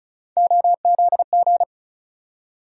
Morse code, heard as a keyed tone, about 36 wpm, 700 Hz. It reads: OZG
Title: USA Call Sign Postfixes - 3 Letters 35wpm